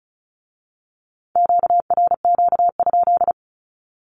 Morse code, heard as a keyed tone, about 36 wpm, 700 Hz. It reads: QRQ?
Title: Q-Codes with Spoken Meaning - 35wpm